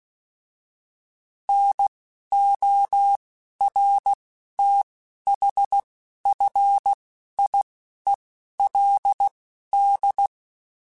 Morse code, heard as a keyed tone, about 16 wpm, spaced as for 11 wpm, 800 Hz. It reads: NORTHFIELD